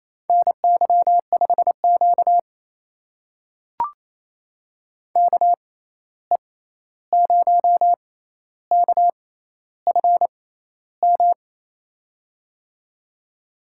Morse code, heard as a keyed tone, about 28 wpm, spaced as for 13 wpm, 700 Hz. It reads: NY5Q KE0KFM
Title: USA Call Signs - 6x Character Spacing - 28wpm